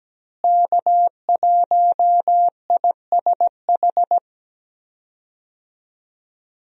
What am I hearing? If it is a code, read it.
Morse code, heard as a keyed tone, about 17 wpm, 700 Hz. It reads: K1ISH
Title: USA Call Signs - 5 Characters - Repeated 3x - 17wpm